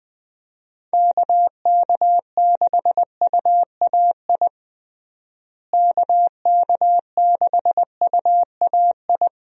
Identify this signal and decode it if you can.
Morse code, heard as a keyed tone, about 20 wpm, 700 Hz. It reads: KK6UAI KK6UAI
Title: USA Call Signs - 6 Characters - Repeated 3x - 20wpm